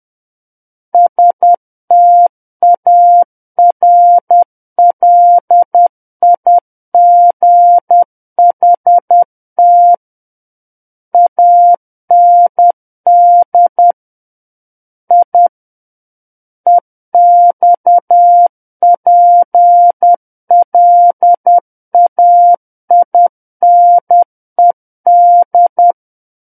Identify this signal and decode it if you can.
Morse code, heard as a keyed tone, about 10 wpm, 700 Hz. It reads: STARLIGHT AND I EXPLAINED
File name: Book One / 01-Chapter 1 - 10 WPM